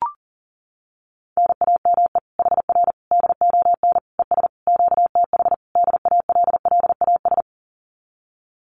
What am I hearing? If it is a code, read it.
Morse code, heard as a keyed tone, about 40 wpm, 700 Hz. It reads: NAME HR DON ES QTH DALLAS
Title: Common QSO Phrases Encore 40wpm